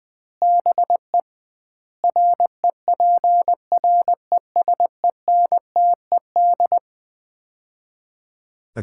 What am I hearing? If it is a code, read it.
Morse code, heard as a keyed tone, about 21 wpm, 700 Hz. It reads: BE REPRESENTED
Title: 2x Word-Spacing - Sets of 5 Words - 20wpm